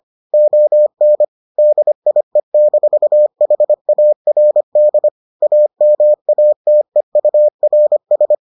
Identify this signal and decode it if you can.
Morse code, heard as a keyed tone, about 25 wpm, 600 Hz. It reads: ON DIE-HARD AMATEURS